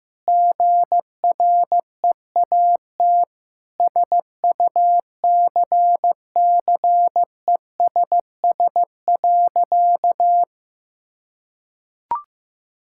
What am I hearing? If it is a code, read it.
Morse code, heard as a keyed tone, about 15 wpm, 700 Hz. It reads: GREAT SUCCESS.
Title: Sentences from Top 1000 Words - Review Part 6 - 15wpm